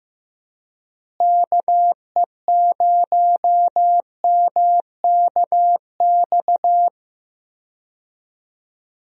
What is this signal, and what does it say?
Morse code, heard as a keyed tone, about 15 wpm, 700 Hz. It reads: KE0MKX
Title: USA Call Signs - 6 Characters - Repeated 3x - 15wpm